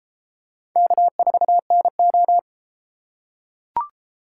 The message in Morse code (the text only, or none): K4NO